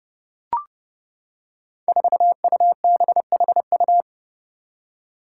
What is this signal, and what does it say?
Morse code, heard as a keyed tone, about 30 wpm, 700 Hz. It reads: 4UBHU
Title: Five Challenging Letter-Number Combinations 30wpm